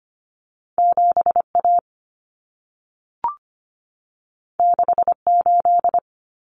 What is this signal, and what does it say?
Morse code, heard as a keyed tone, about 25 wpm, 700 Hz. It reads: 7A 68
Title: Common Cut Numbers - 2-Digit 25wpm